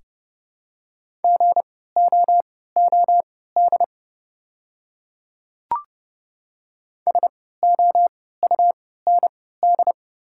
Morse code, heard as a keyed tone, about 30 wpm, spaced as for 17 wpm, 700 Hz. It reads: GOOD SOUND
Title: 3x Character Spacing - Top 200 Words - 30wpm